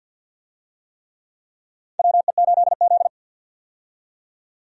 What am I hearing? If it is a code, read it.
Morse code, heard as a keyed tone, about 47 wpm, 700 Hz. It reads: WE8Z
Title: USA Call Signs - 4 Characters - 50wpm